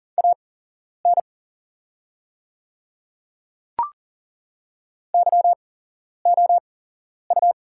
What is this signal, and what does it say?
Morse code, heard as a keyed tone, about 40 wpm, spaced as for 11 wpm, 700 Hz. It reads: AN YOU
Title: 8x Character Spacing - Top 100 Words - 40wpm